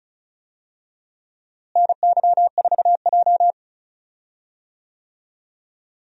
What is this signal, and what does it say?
Morse code, heard as a keyed tone, about 36 wpm, 700 Hz. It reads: NY4J
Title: USA Call Signs - 4 Characters - Repeated 3x - 35wpm